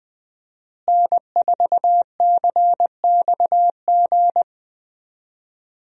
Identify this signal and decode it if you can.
Morse code, heard as a keyed tone, about 21 wpm, 700 Hz. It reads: N4CXG
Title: USA Call Signs - 5 Characters - Repeated 3x - 20wpm